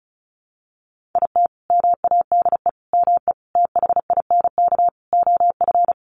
Morse code, heard as a keyed tone, about 34 wpm, 700 Hz. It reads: IT MADE ME THINK OF
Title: Sentences from Top 1100 Words 35wpm